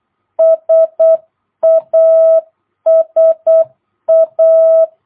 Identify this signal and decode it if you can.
Morse code, heard as a keyed tone, about 8 wpm, 660 Hz. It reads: SASA
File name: jh8cbh / 2022cw